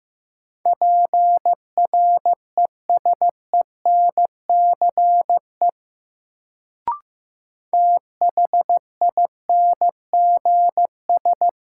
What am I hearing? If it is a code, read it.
Morse code, heard as a keyed tone, about 15 wpm, 700 Hz. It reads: PRESENCE THINGS
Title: Sets of 3 Words - New Word Drill - 15wpm